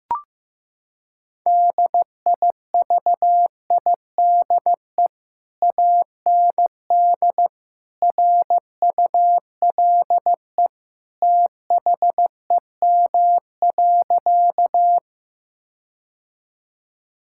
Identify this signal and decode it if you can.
Morse code, heard as a keyed tone, about 15 wpm, 700 Hz. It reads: DIVIDE AND RULE THEM.
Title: Sentences from Top 1600 Words 15wpm